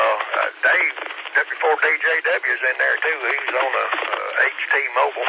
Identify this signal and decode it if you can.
Emergency Alert System (SAME) tones (header bursts), heard as harsh buzzing bursts, with speech. It is heard from 0.5 to 4.9 s.